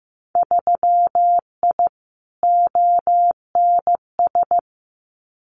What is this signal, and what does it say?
Morse code, heard as a keyed tone, about 15 wpm, 700 Hz. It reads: 3I ONS
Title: Winter Field Day Exchanges Encore 15wpm